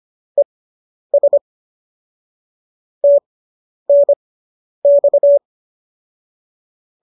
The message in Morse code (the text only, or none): ES TNX